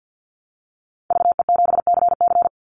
Morse code, heard as a keyed tone, about 49 wpm, 700 Hz. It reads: VE7CC
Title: Call Signs - 50wpm